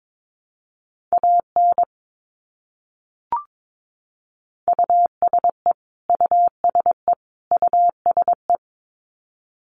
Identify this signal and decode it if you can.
Morse code, heard as a keyed tone, about 22 wpm, 700 Hz. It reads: AN USE USE USE
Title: Top 200 Words Repeated 22wpm